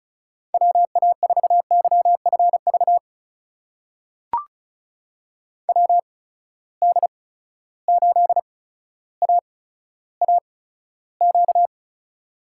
Morse code, heard as a keyed tone, about 36 wpm, spaced as for 15 wpm, 700 Hz. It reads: WA4YFV WD8AAQ
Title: USA Call Signs - 6 Characters - 8x Character Spacing - 35wpm